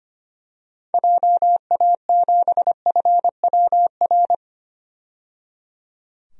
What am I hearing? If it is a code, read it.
Morse code, heard as a keyed tone, about 25 wpm, 700 Hz. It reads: JA7FWR
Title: DXCC-100 Call Signs - 6 Characters - 25wpm